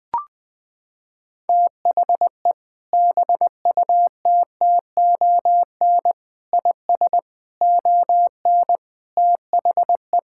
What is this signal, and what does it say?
Morse code, heard as a keyed tone, about 19 wpm, 700 Hz. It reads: THE BUTTON IS ON THE
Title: Sentences with 6-Word Phrases - 20wpm